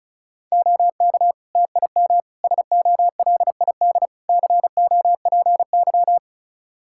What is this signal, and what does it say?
Morse code, heard as a keyed tone, about 36 wpm, 700 Hz. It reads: OK TIM SOLID COPY
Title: Practice QSO - Exchange 3 - More Info - 35wpm